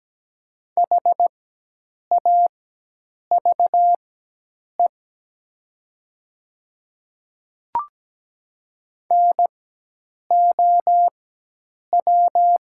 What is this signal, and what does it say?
Morse code, heard as a keyed tone, about 17 wpm, spaced as for 8 wpm, 700 Hz. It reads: HAVE NOW
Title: 4x Character Spacing - Top 100 Words - 17wpm